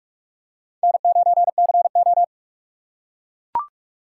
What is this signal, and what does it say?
Morse code, heard as a keyed tone, about 44 wpm, 700 Hz. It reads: N9CO
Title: USA Call Signs - 4 Characters - 4x Character Spacing - 45wpm